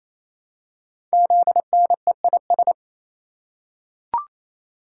Morse code, heard as a keyed tone, about 27 wpm, 700 Hz. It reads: ZNEIS
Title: Code Groups - 5 Letters - 28wpm